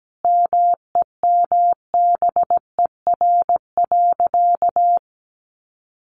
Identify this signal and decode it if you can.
Morse code, heard as a keyed tone, about 17 wpm, 700 Hz. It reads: MEMBER.
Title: Sentences from Top 1200 Words 17wpm